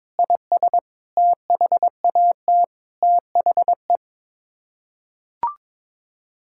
Morse code, heard as a keyed tone, about 21 wpm, 700 Hz. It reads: IS THAT THE
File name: practice / 22wpm / Sets of 7 Words - 22wpm